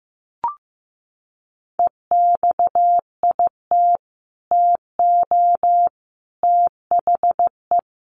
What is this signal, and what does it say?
Morse code, heard as a keyed tone, about 15 wpm, 700 Hz. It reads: EXIT TO THE
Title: Sentences from Top 1100 Words 15wpm